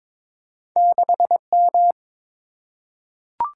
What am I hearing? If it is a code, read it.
Morse code, heard as a keyed tone, about 22 wpm, 690 Hz. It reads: 6M
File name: practice / 22wpm / Two Letter-Number Combinations - 22wpm